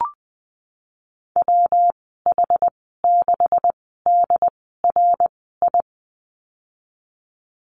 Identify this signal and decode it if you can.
Morse code, heard as a keyed tone, about 19 wpm, spaced as for 14 wpm, 700 Hz. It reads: WH6DRI
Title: 128 Lesson 21g - Number 6 - Call Signs 20wpm